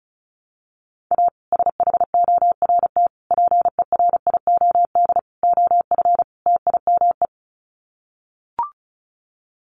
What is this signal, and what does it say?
Morse code, heard as a keyed tone, about 34 wpm, 700 Hz. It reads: A SHORT PERIOD OF TIME
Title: Sets of 5 Words Encore 35wpm